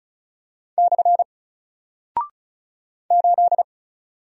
Morse code, heard as a keyed tone, about 36 wpm, 700 Hz. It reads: / 8
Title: Single Character (Letter:Number:Punctuation:Prosign) Alt 35wpm